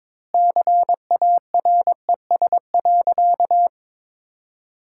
Morse code, heard as a keyed tone, about 22 wpm, 700 Hz. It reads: CARES.
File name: practice / 22wpm / Sentences with 3-Word Phrases - 22wpm